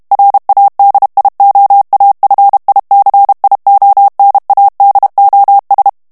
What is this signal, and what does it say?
Morse code, heard as a keyed tone, about 32 wpm, 800 Hz. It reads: RADIOAFICIONADOS
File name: morse mp3 / ESPAGNOL TEXTE EN CLAIR 1 VITESSE 30 MOTS MINUTES